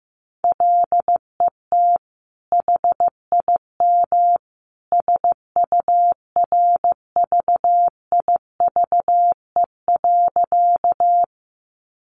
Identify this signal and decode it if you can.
Morse code, heard as a keyed tone, about 15 wpm, 700 Hz. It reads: LET HIM SURVIVE.
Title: Sentences from Top 1300 Words 15wpm